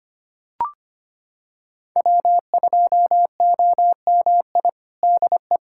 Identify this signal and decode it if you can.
Morse code, heard as a keyed tone, about 25 wpm, 700 Hz. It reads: W2OMI DE
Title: USA Callsign DE Callsign - 5 Characters - 25wpm